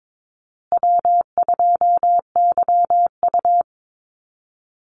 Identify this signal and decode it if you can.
Morse code, heard as a keyed tone, about 22 wpm, 700 Hz. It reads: W2YU
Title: USA Call Signs Repeated 3x 22wpm